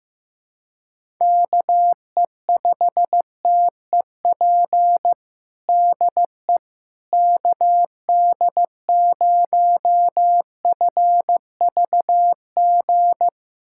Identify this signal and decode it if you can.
Morse code, heard as a keyed tone, about 15 wpm, 700 Hz. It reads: KE5TEP DE KD0FVG
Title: USA Callsign DE Callsign - 6 Characters - 15wpm